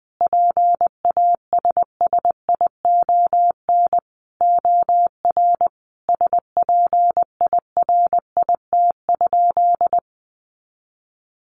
Morse code, heard as a keyed tone, about 20 wpm, 700 Hz. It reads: PASSION OR SPIRIT?
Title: Sentences from Top 1400 Words 20wpm